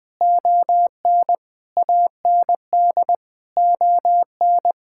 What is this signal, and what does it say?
Morse code, heard as a keyed tone, about 21 wpm, 700 Hz. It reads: ON AND ON
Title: Sets of 8 Words - 20wpm